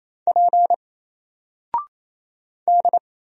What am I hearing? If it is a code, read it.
Morse code, heard as a keyed tone, about 28 wpm, 700 Hz. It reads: P D
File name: Single-Letter-Number-28wpm